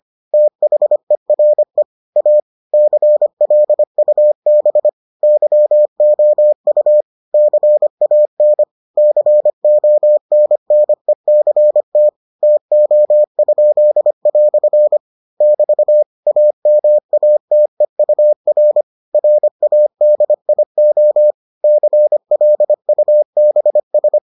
Morse code, heard as a keyed tone, about 25 wpm, 600 Hz. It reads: THERE A CLUB YOU CAN CONNECT TO?" <BT> AMATEUR RADIO CLUBS